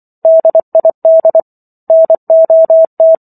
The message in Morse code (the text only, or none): DID NOT